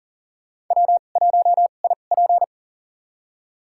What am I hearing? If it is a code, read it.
Morse code, heard as a keyed tone, about 40 wpm, spaced as for 29 wpm, 700 Hz. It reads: W1IP
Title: USA Call Signs - 4 Characters - 2x Character Spacing - 40wpm